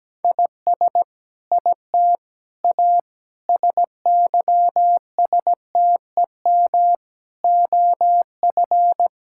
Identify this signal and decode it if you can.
Morse code, heard as a keyed tone, about 17 wpm, 700 Hz. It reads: IS IT A SYSTEM OF
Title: Sentences from Top 1000 Words - Review Part 6 - 17wpm